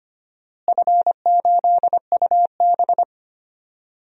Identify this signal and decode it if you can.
Morse code, heard as a keyed tone, about 25 wpm, 700 Hz. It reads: F8UB
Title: DXCC-100 Call Signs - 4 Characters - Repeated 2x - 25wpm